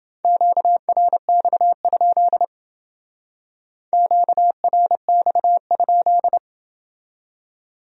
Morse code, heard as a keyed tone, about 28 wpm, 700 Hz. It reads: QRX? QRX?